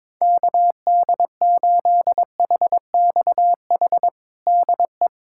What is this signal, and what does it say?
Morse code, heard as a keyed tone, about 22 wpm, 700 Hz. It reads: KD8HXH DE
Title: USA Callsign DE Callsign 22wpm